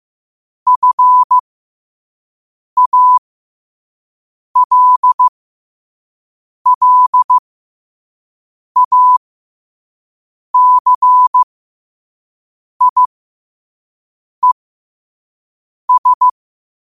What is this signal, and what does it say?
Morse code, heard as a keyed tone, about 15 wpm, spaced as for 5 wpm, 1000 Hz. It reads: FALLACIES
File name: qotdfiles / 5wpm